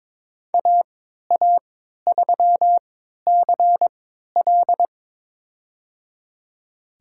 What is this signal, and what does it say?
Morse code, heard as a keyed tone, about 22 wpm, spaced as for 13 wpm, 700 Hz. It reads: AA3CL
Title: USA Call Signs - 5 Characters - 3x Character Spacing - 22wpm